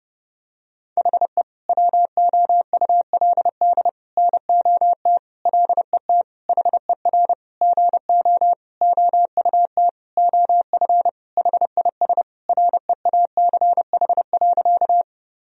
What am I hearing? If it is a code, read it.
Morse code, heard as a keyed tone, about 32 wpm, 700 Hz. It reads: HE WOULD NOT LET HER GO OUT OF HIS REACH.
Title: Sentences from Top 1000 Words - Review Part 2 - 30wpm